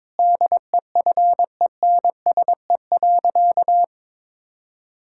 Morse code, heard as a keyed tone, about 22 wpm, 700 Hz. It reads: DEFENSE.
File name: Difficult Sentences with Hints - 22wpm